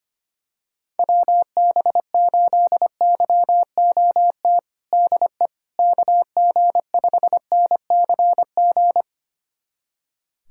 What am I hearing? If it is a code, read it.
Morse code, heard as a keyed tone, about 25 wpm, 700 Hz. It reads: WB8YOT DE KG5NCG
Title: USA Callsign DE Callsign 25wpm